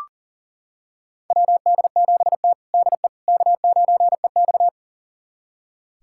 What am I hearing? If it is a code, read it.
Morse code, heard as a keyed tone, about 40 wpm, 690 Hz. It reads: WD7T DE K9EX
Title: USA Callsign DE Callsign - 4 Characters - 40wpm